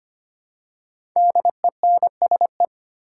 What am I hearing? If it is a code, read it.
Morse code, heard as a keyed tone, about 25 wpm, 700 Hz. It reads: DENSE
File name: 5 Letter Words - First 3 Letters Given - 25wpm